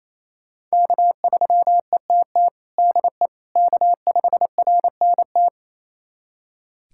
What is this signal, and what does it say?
Morse code, heard as a keyed tone, about 28 wpm, 700 Hz. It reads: K3ETT DE K5RNT
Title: USA Callsign DE Callsign - 5 Characters - 28wpm